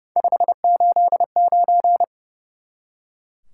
Morse code, heard as a keyed tone, about 30 wpm, 700 Hz. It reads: 589
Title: Top 180 QSO Elements 30wpm